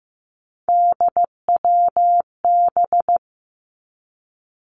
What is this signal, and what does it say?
Morse code, heard as a keyed tone, about 15 wpm, 700 Hz. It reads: DWB
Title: Three Challenging Letter-Number Combinations 15wpm